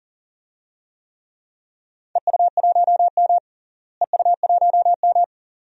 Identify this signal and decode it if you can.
Morse code, heard as a keyed tone, about 40 wpm, 700 Hz. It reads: EU1M EU1M
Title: DXCC-100 Call Signs - 4 Characters - Repeated 2x - 40wpm